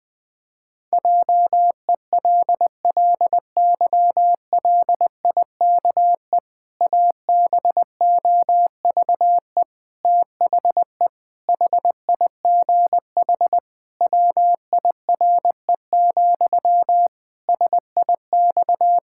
Morse code, heard as a keyed tone, about 21 wpm, 700 Hz. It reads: JELLYLIKE ABOVE THE HIGH WIRE, SIX
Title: Pangram Challenge v2 - 20wpm